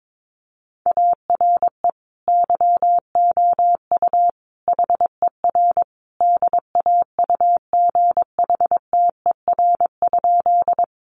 Morse code, heard as a keyed tone, about 22 wpm, 700 Hz. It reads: ARE YOU HER DAUGHTER?
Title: Sentences from Top 1500 Words 22wpm